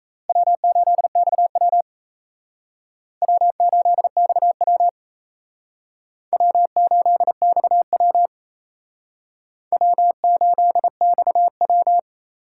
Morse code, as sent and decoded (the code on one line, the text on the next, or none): .-- ---.. -..- .-- / .-- ---.. -..- .-- / .-- ---.. -..- .-- / .-- ---.. -..- .--
W8XW W8XW W8XW W8XW